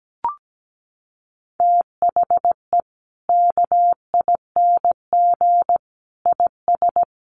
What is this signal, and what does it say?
Morse code, heard as a keyed tone, about 17 wpm, 700 Hz. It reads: THE KING IS
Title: Sentences from Top 600 Words 17wpm